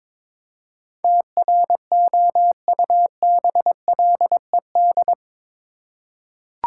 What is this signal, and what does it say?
Morse code, heard as a keyed tone, about 22 wpm, 700 Hz. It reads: TROUBLED
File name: practice / 22wpm / Eight Letter Words - 22wpm